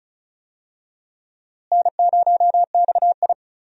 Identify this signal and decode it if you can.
Morse code, heard as a keyed tone, about 34 wpm, 700 Hz. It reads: N0XI
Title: USA Call Signs - 4 Characters - Repeated 3x - 35wpm